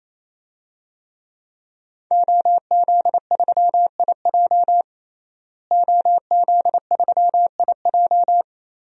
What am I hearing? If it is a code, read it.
Morse code, heard as a keyed tone, about 28 wpm, 700 Hz. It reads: OZ3IJ OZ3IJ
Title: DXCC-100 Call Signs - 5 Characters - Repeated 3x - 28wpm